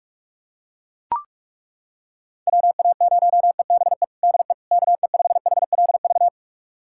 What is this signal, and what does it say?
Morse code, heard as a keyed tone, about 42 wpm, 700 Hz. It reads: WA0EBE DE KE5HLV
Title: USA Callsign DE Callsign - 6 Characters - 45wpm